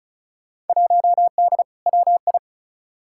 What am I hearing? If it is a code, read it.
Morse code, heard as a keyed tone, about 36 wpm, 700 Hz. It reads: 1D WI
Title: Field Day Exchanges - 35wpm